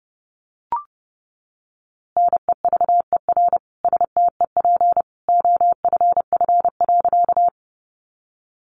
Morse code, heard as a keyed tone, about 30 wpm, 700 Hz. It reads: NEVER STEP OFF.
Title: A Sound Of Thunder 30wpm